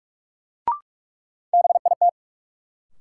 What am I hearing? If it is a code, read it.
Morse code, heard as a keyed tone, about 46 wpm, 700 Hz. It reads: BIT
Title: Sets of 3 Words - New Word Drill - 45wpm